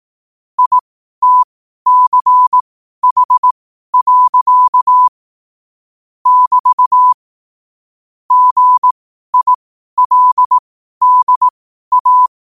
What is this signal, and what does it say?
Morse code, heard as a keyed tone, about 18 wpm, spaced as for 12 wpm, 1000 Hz. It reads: ITCH. <BT> GILDA
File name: qotdfiles / 12wpm